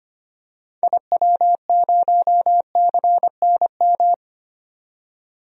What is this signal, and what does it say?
Morse code, heard as a keyed tone, about 25 wpm, 700 Hz. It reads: IW0CNM
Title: DXCC-100 Call Signs - 6 Characters - 25wpm